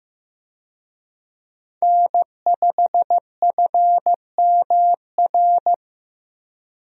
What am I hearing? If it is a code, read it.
Morse code, heard as a keyed tone, about 15 wpm, 700 Hz. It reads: N5FMR
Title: USA Call Signs - 5 Characters - Repeated 3x - 15wpm